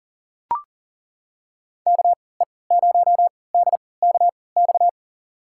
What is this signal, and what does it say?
Morse code, heard as a keyed tone, about 40 wpm, spaced as for 23 wpm, 700 Hz. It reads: KE0DKX
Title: USA Call Signs - 6 Characters - 3x Character Spacing - 40wpm